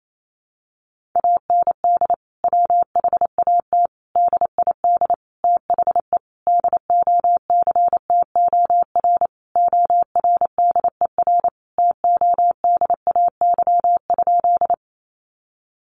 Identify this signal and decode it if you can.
Morse code, heard as a keyed tone, about 28 wpm, 700 Hz. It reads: AND WHAT DID THE DOCTOR ORDER TODAY?
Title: Sentences from Top 1500 Words 28wpm